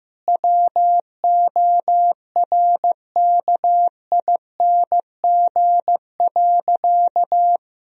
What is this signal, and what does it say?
Morse code, heard as a keyed tone, about 15 wpm, 700 Hz. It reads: WORKING.
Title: Sentences from Top 1000 Words - Review Part 4 - 15wpm